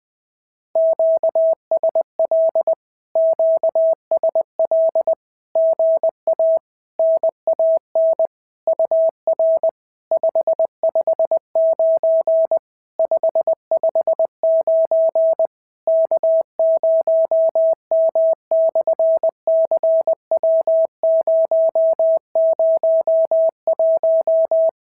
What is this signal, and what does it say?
Morse code, heard as a keyed tone, about 20 wpm, 690 Hz. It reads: QSL QSL GA NAN UR 559 559 K0M/CW001
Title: SOTA QSO Practice 20wpm